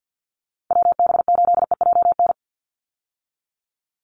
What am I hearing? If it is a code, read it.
Morse code, heard as a keyed tone, about 50 wpm, 700 Hz. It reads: WB8EJN
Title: Call Signs Repeated 2x - 50wpm